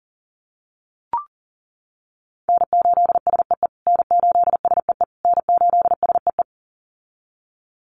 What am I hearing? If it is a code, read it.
Morse code, heard as a keyed tone, about 40 wpm, 700 Hz. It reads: N8SEE N8SEE N8SEE
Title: USA Call Signs Repeated 3x 40wpm